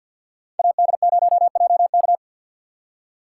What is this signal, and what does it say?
Morse code, heard as a keyed tone, about 64 wpm, 700 Hz. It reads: AD0JK